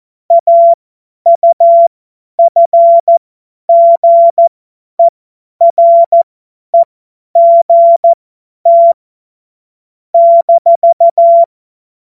Morse code, heard as a keyed tone, about 14 wpm, spaced as for 10 wpm, 680 Hz. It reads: AUFGEREGT -
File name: Morsekurs / Witze / Witz 15. August 2023 50 F70